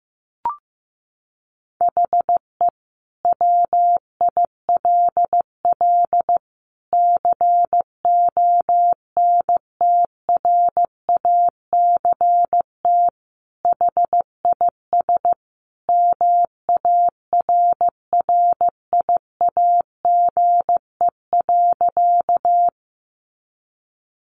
Morse code, heard as a keyed tone, about 15 wpm, 700 Hz. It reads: HE WILL CONTRACT HIS MARRIAGE.
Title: Sentences from Top 1400 Words 15wpm